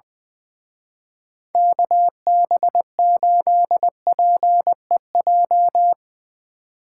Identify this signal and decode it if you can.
Morse code, heard as a keyed tone, about 23 wpm, 700 Hz. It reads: KB8PEJ
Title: Speed Racing - USA 2x3 Call Signs 20wpm